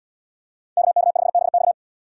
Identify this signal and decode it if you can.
Morse code, heard as a keyed tone, about 73 wpm, 700 Hz. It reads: KK5BX